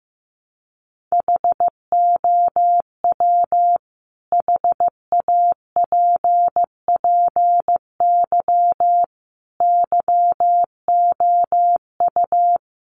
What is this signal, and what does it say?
Morse code, heard as a keyed tone, about 15 wpm, 700 Hz. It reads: HOW HAPPY YOU
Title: Sentences from Top 1000 Words 15wpm